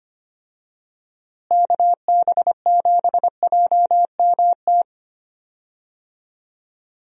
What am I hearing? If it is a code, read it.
Morse code, heard as a keyed tone, about 25 wpm, 700 Hz. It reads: KB7JMT
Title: USA Call Signs - 6 Characters - Repeated 3x - 25wpm